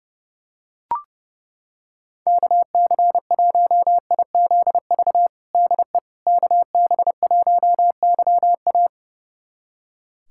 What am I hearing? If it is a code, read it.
Morse code, heard as a keyed tone, about 30 wpm, 700 Hz. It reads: KC1IZV DE KB1YA
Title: USA Callsign DE Callsign 30wpm